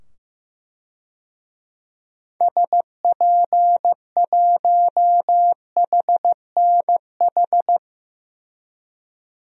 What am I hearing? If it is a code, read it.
Morse code, heard as a keyed tone, about 15 wpm, 700 Hz. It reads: SP1HNH
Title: DXCC-100 Call Signs - 6 Characters - 15wpm